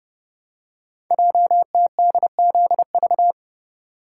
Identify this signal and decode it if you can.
Morse code, heard as a keyed tone, about 30 wpm, 700 Hz. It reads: JTDZV
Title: Code Groups - 5 Letters - 30wpm